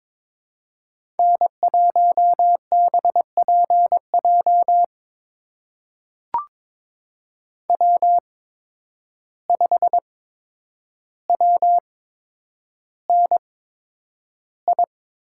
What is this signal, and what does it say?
Morse code, heard as a keyed tone, about 22 wpm, spaced as for 9 wpm, 700 Hz. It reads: N1BPJ W5WNI
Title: USA Call Signs - 5 Characters - 8x Character Spacing - 22wpm